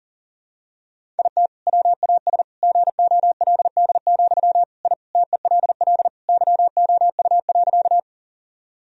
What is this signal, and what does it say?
Morse code, heard as a keyed tone, about 40 wpm, 700 Hz. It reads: IT WAS GOLD, I TELL YOU.
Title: Sentences from Top 1000 Words - Review Part 3 - 40wpm